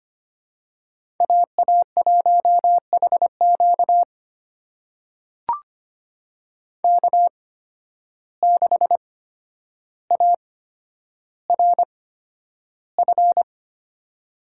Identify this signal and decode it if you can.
Morse code, heard as a keyed tone, about 25 wpm, spaced as for 11 wpm, 700 Hz. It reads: AA1HQ K6ARF
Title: USA Call Signs - 5 Characters - 8x Character Spacing - 25wpm